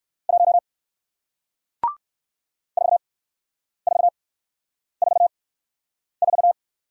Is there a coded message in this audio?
Yes: Morse code from 0.3 to 6.5 s, heard as a keyed tone, about 59 wpm, spaced as for 9 wpm, 700 Hz. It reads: 1 4444